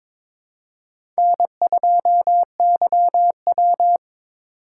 Morse code, heard as a keyed tone, about 22 wpm, 700 Hz. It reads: N2YW
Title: USA Call Signs - 4 Characters - 2x Character Spacing - 22wpm